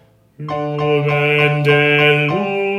There is no code.